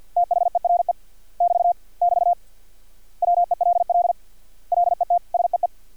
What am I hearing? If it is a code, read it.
Morse code, heard as a keyed tone, about 50 wpm, 700 Hz. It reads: THERE <BT> <BT> WELL LET SEE
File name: archived / CRSnet